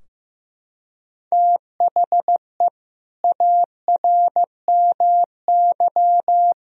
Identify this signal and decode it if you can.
Morse code, heard as a keyed tone, about 15 wpm, 700 Hz. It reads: THE ARMY
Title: Sentences from Top 1000 Words - Review Part 4 - 15wpm